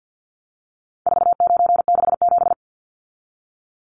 Morse code, heard as a keyed tone, about 50 wpm, 700 Hz. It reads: 4967
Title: Numbers - 4-Digit 50wpm